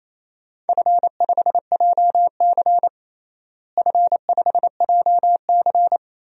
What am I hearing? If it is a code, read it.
Morse code, heard as a keyed tone, about 28 wpm, 700 Hz. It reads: F5JC F5JC
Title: DXCC-100 Call Signs - 4 Characters - Repeated 3x - 28wpm